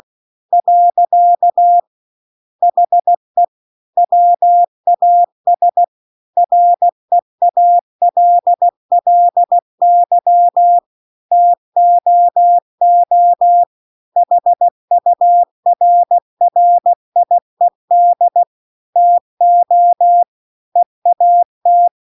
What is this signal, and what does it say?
Morse code, heard as a keyed tone, about 16 wpm, 700 Hz. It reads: . HE WAS REALLY TOO HURRIED TO EAT
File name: 16WPM